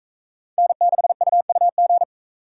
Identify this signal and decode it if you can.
Morse code, heard as a keyed tone, about 39 wpm, 700 Hz. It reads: N6UUG